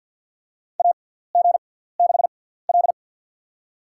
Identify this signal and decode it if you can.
Morse code, heard as a keyed tone, about 49 wpm, spaced as for 17 wpm, 690 Hz. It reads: AG6L
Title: USA Call Signs - 4 Characters - 6x Character Spacing - 50wpm